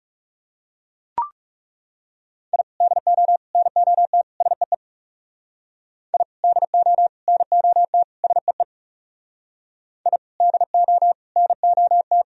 Morse code, heard as a keyed tone, about 37 wpm, 700 Hz. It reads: I DO NOT SEE I DO NOT SEE I DO NOT
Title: Speed-Racing - Sets of 4 Words 30wpm